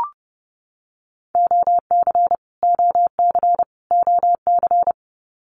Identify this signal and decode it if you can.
Morse code, heard as a keyed tone, about 30 wpm, 700 Hz. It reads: OC OC OC
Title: Abbreviations Repeated 3x - 30wpm